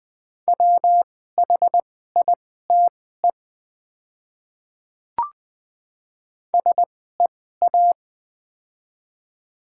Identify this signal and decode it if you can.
Morse code, heard as a keyed tone, about 20 wpm, spaced as for 14 wpm, 700 Hz. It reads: WHITE SEA
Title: 2x Character Spacing - Top 300 Words - 20wpm